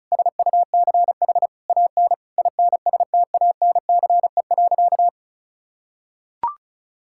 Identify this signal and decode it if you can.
Morse code, heard as a keyed tone, about 34 wpm, 700 Hz. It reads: SUCH AN INSTANCE.
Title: Sentences from Top 1000 Words - Review Part 5 - 35wpm